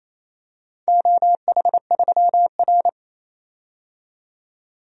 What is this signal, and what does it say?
Morse code, heard as a keyed tone, about 28 wpm, 700 Hz. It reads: OH3R
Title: DXCC-100 Call Signs - 4 Characters - 28wpm